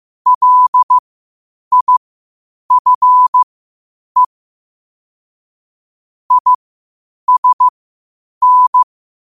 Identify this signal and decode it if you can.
Morse code, heard as a keyed tone, about 15 wpm, spaced as for 8 wpm, 1000 Hz. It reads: LIFE ISN
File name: qotdfiles / 8wpm